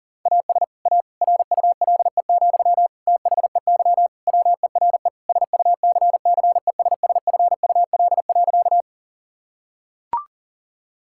Morse code, heard as a keyed tone, about 40 wpm, 700 Hz. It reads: AS A RULE, THEY WERE SUCCESSFUL.
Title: Sentences from Top 1000 Words - Review Part 6 - 40wpm